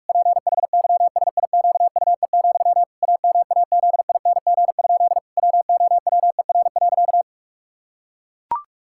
Morse code, heard as a keyed tone, about 41 wpm, 700 Hz. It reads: PHYSIQUE, AMAZING? WOWER.
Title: Pangram Challenge v2 - 45wpm